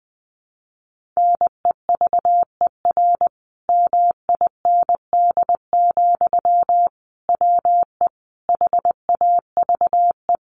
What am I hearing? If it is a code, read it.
Morse code, heard as a keyed tone, about 20 wpm, 700 Hz. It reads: NEVER MIND, WE HAVE
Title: Sentences from Top 1300 Words 20wpm